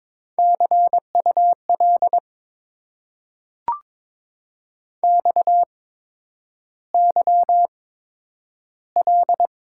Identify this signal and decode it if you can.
Morse code, heard as a keyed tone, about 22 wpm, spaced as for 9 wpm, 690 Hz. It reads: CUL XYL
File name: Top 90 QSO Elements - 8x Character Spacing - 22wpm